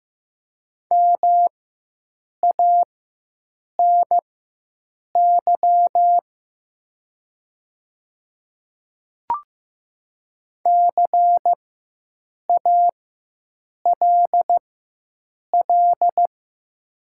Morse code, heard as a keyed tone, about 15 wpm, spaced as for 7 wpm, 700 Hz. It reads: MANY CALL
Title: 4x Character Spacing - Top 100 Words - 15wpm